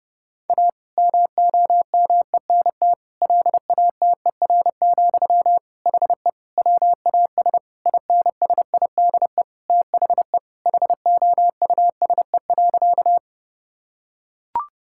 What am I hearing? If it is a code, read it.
Morse code, heard as a keyed tone, about 28 wpm, 700 Hz. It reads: A MOMENT LATER, HE WAS INSIDE THE HOUSE.
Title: Sentences from Top 1000 Words - Review Part 6 - 30wpm